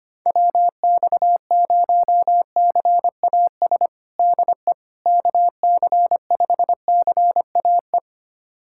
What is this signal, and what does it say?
Morse code, heard as a keyed tone, about 25 wpm, 700 Hz. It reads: WX0CAS DE KC5CAE
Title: USA Callsign DE Callsign - 6 Characters - 25wpm